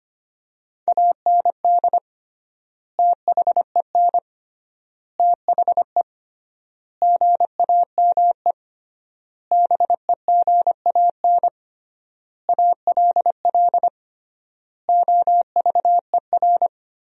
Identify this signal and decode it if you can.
Morse code, heard as a keyed tone, about 25 wpm, 700 Hz. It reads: AND THEN THE GAME BEGAN ALL OVER
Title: 3x Word-Spacing - Sentences from Top 500 Words - 25wpm